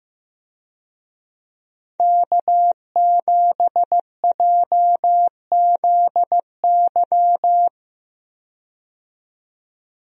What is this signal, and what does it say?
Morse code, heard as a keyed tone, about 15 wpm, 700 Hz. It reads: K7JZY
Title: USA Call Signs - 5 Characters - 15wpm